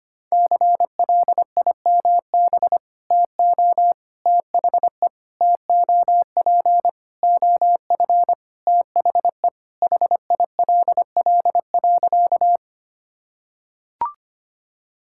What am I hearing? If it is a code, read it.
Morse code, heard as a keyed tone, about 25 wpm, 700 Hz. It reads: CLIMB TO THE TOP OF THE HILL.
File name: practice / 25wpm / Sentences with 4-Word Phrases - 25wpm